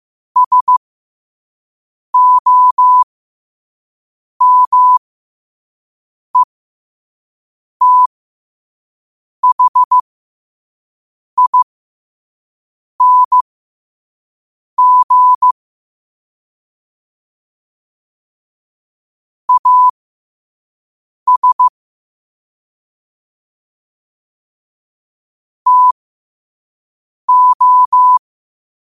Morse code, heard as a keyed tone, about 15 wpm, spaced as for 5 wpm, 1000 Hz. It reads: SOMETHING AS TO
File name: qotdfiles / 5wpm